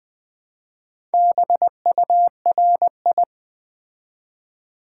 Morse code, heard as a keyed tone, about 20 wpm, 700 Hz. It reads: BURI
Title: Build A Word - 2 Letters At A Time - 6-Letter Words - 20wpm